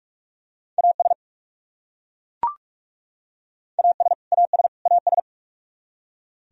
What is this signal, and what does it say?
Morse code, heard as a keyed tone, about 41 wpm, 700 Hz. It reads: AS AS AS AS
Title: Top 100 Words Repeated 45wpm